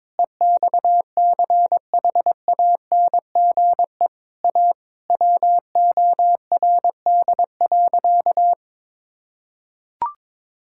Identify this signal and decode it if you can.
Morse code, heard as a keyed tone, about 21 wpm, 700 Hz. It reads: EXCHANGE A WORD.
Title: Sentences from Top 1000 Words - Review Part 6 - 22wpm